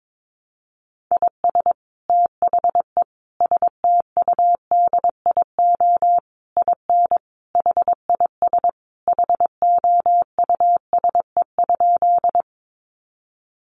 Morse code, heard as a keyed tone, about 22 wpm, 700 Hz. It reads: IS THE STUDIO IN HIS HOUSE?
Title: Sentences from Top 1200 Words 22wpm